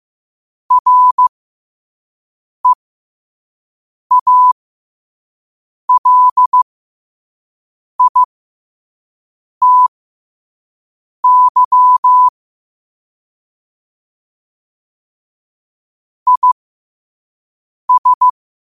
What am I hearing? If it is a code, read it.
Morse code, heard as a keyed tone, about 15 wpm, spaced as for 5 wpm, 1000 Hz. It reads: REALITY IS
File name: qotdfiles / 5wpm